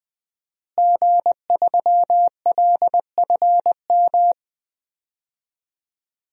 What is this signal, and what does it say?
Morse code, heard as a keyed tone, about 21 wpm, 700 Hz. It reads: G3LFM
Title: DXCC-100 Call Signs - 5 Characters - Repeated 3x - 20wpm